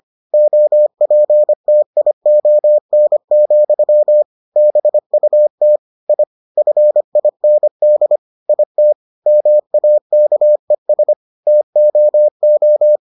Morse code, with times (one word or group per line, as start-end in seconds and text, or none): 0.3-4.2 OPTION,
4.6-5.8 BUT
6.1-6.2 I
6.6-8.2 FIND
8.5-8.9 IT
9.3-11.1 MAKES
11.5-13.0 TOO